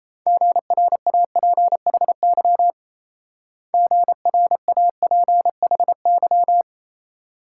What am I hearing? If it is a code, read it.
Morse code, heard as a keyed tone, about 34 wpm, 700 Hz. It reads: GRAPHY GRAPHY
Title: Speed-Racing Multi-letter Phonemes - 28wpm